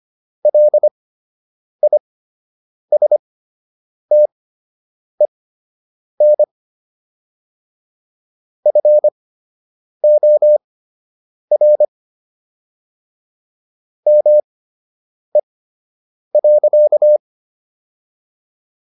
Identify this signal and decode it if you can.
Morse code, heard as a keyed tone, about 25 wpm, spaced as for 8 wpm, 600 Hz. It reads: LISTEN FOR ME.